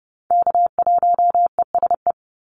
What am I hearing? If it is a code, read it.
Morse code, heard as a keyed tone, about 30 wpm, 700 Hz. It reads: K1ESE
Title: Call Signs - 30wpm